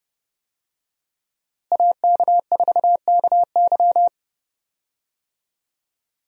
Morse code, heard as a keyed tone, about 32 wpm, 700 Hz. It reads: AK4KY
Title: USA Call Signs - 5 Characters - Repeated 3x - 30wpm